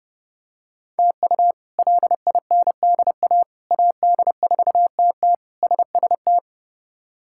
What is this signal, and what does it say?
Morse code, heard as a keyed tone, about 30 wpm, 700 Hz. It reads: TU LINDA AD4TT SST
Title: K1USN SST QSO Practice - 30wpm